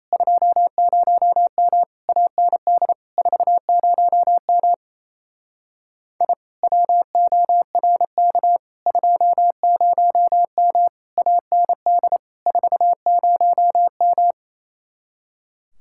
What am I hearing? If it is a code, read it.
Morse code, heard as a keyed tone, about 31 wpm, 700 Hz. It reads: 20M AND 40M I WORK 20M AND 40M
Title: Speed-Racing Common QSO Phrases Encore - 28wpm